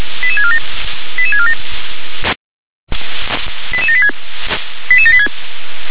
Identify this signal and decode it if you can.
Five-tone selective call (ZVEI), heard as quick runs of five tones, tones of about 70 ms. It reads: 88647 88647 88786 88786
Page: /357/SDR/ZVEI/88786/2023.06.19/